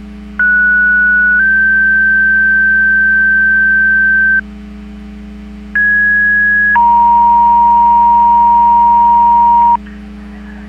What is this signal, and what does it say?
Two-tone page, heard as a one-second tone, then a three-second tone, about 1470 then 1550 Hz, about 1650 then 940 Hz. It is heard from 0.4 to 9.8 s.
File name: Portage 154.385 Paging